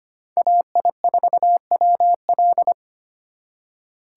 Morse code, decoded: AI4WL

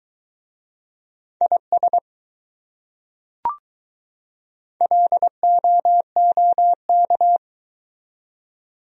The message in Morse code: IS LOOK

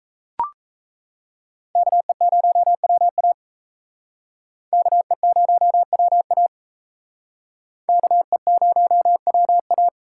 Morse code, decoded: KE0WA KE0WA KE0WA

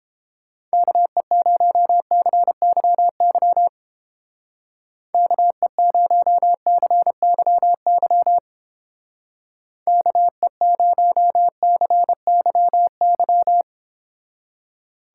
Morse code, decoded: KE0CYY KE0CYY KE0CYY

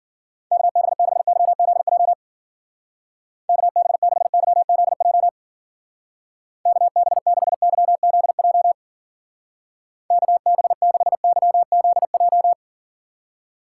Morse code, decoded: KB6YZJ KB6YZJ KB6YZJ KB6YZJ

